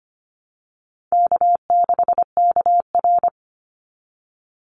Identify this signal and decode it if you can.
Morse code, heard as a keyed tone, about 25 wpm, 700 Hz. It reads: K6KR
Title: Easy Call Signs Repeated 3x - 25wpm